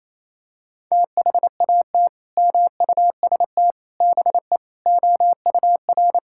Morse code, heard as a keyed tone, about 28 wpm, 700 Hz. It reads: THAT MUST BE OUR
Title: Sentences from Top 1000 Words - Review Part 5 - 28wpm